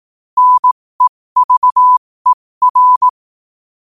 Morse code, heard as a keyed tone, about 18 wpm, spaced as for 15 wpm, 1000 Hz. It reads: NEVER
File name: qotdfiles / 15wpm